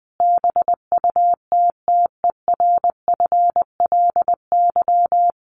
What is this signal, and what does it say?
Morse code, heard as a keyed tone, about 19 wpm, 700 Hz. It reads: BUTTERFLY
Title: A Sound Of Thunder 20wpm